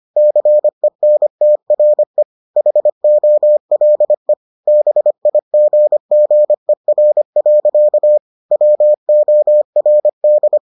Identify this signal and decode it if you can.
Morse code, heard as a keyed tone, about 25 wpm, 600 Hz. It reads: CENTRE HOLE BIGGER. WORD